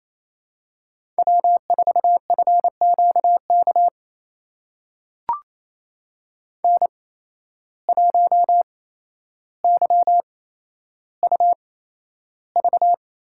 Morse code, heard as a keyed tone, about 28 wpm, spaced as for 12 wpm, 700 Hz. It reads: W4FQK N1YUV
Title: USA Call Signs - 5 Characters - 8x Character Spacing - 28wpm